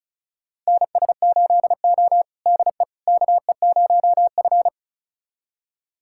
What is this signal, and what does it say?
Morse code, heard as a keyed tone, about 34 wpm, 700 Hz. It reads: NS8O DE KE0F
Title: USA Callsign DE Callsign - 4 Characters - 35wpm